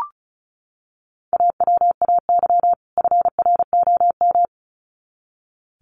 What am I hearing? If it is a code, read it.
Morse code, heard as a keyed tone, about 34 wpm, 700 Hz. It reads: AWAY FROM
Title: Sets of 2 Words 35wpm